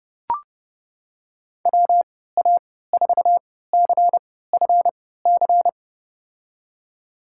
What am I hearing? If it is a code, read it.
Morse code, heard as a keyed tone, about 32 wpm, spaced as for 17 wpm, 700 Hz. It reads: WA4CFC
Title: USA Call Signs - 6 Characters - 3x Character Spacing - 30wpm